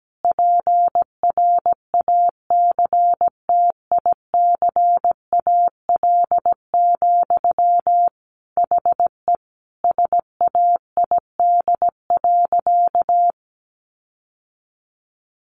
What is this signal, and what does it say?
Morse code, heard as a keyed tone, about 17 wpm, 700 Hz. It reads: PRACTICAL, HE SAID.
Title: Sentences from Top 1400 Words 17wpm